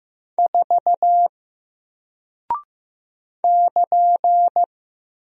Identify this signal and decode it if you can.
Morse code, heard as a keyed tone, about 15 wpm, 700 Hz. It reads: 4 <KN>
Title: Single Character (Letter:Number:Punctuation:Prosign) 15wpm